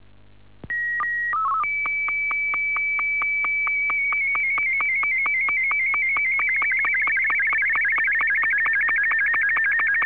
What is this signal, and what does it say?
SSTV (slow-scan television), Martin 2.